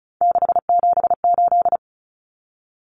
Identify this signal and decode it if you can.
Morse code, heard as a keyed tone, about 34 wpm, 700 Hz. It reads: 678
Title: Common Cut Numbers - 3-Digit 35wpm